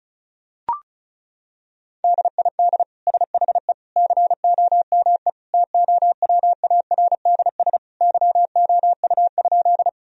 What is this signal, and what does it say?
Morse code, heard as a keyed tone, about 36 wpm, 700 Hz. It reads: DID SHE COME TOWARDS YOU?
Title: Sentences from Top 1000 Words - Review Part 2 - 35wpm